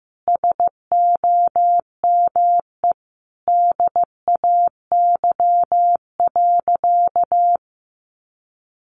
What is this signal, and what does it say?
Morse code, heard as a keyed tone, about 15 wpm, 700 Hz. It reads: SOME DAY.
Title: Sentences from Top 900 Words 15wpm